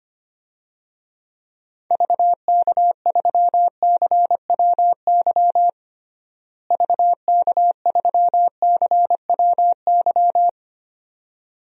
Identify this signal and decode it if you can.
Morse code, heard as a keyed tone, about 25 wpm, 700 Hz. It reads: VK3CWY VK3CWY
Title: DXCC-100 Call Signs - 6 Characters - Repeated 2x - 25wpm